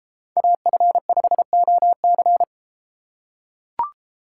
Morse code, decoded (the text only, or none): AF5OC